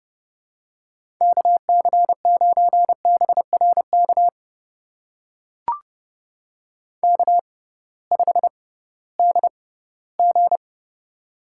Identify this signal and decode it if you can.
Morse code, heard as a keyed tone, about 29 wpm, spaced as for 17 wpm, 700 Hz. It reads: KC9BRK K5DG
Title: USA Call Signs - 6x Character Spacing - 30wpm